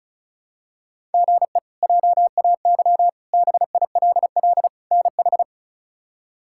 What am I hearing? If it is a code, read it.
Morse code, heard as a keyed tone, about 34 wpm, 700 Hz. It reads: GE JAY BILL NH
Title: K1USN SST QSO Practice - 35wpm